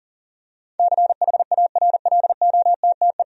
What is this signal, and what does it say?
Morse code, heard as a keyed tone, about 40 wpm, 700 Hz. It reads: CHARLOTTE
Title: USA Common Names - 40wpm